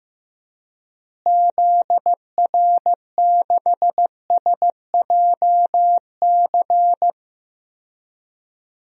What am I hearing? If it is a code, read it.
Morse code, heard as a keyed tone, about 16 wpm, 700 Hz. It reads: ZR6SJC
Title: DXCC-100 Call Signs - 6 Characters - Repeated 3x - 15wpm